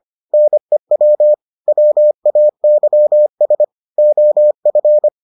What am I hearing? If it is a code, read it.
Morse code, heard as a keyed tone, about 25 wpm, 600 Hz. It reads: NEW WAYS OF